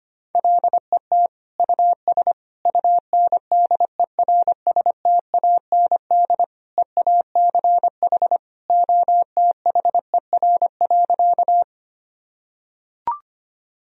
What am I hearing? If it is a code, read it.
Morse code, heard as a keyed tone, about 25 wpm, 700 Hz. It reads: LET US UNDERSTAND EACH OTHER.
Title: Sentences from Top 1000 Words - Review Part 1 - 25wpm